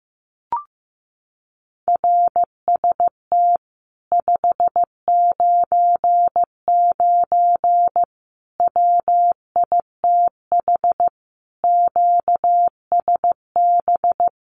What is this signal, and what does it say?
Morse code, heard as a keyed tone, about 15 wpm, 700 Hz. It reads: RST 599 WITH QSB
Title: Common QSO Phrases Encore 15wpm